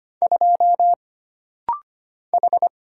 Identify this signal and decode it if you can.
Morse code, heard as a keyed tone, about 25 wpm, 700 Hz. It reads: 2 H
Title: Single Letter-Number - Rapid-Fire 25wpm